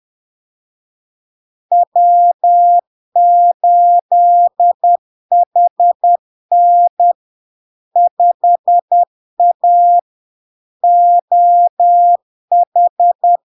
Morse code, heard as a keyed tone, about 10 wpm, 700 Hz. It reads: W8HN 5A OH
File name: LARK-FD-2022-Log-in-CW-10wpm-700-Hz